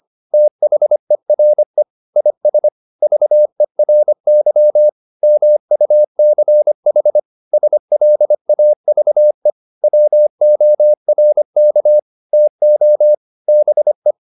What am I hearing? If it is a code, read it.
Morse code, heard as a keyed tone, about 25 wpm, 600 Hz. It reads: THERE IS VERY MUCH SLAVE WORK TO BE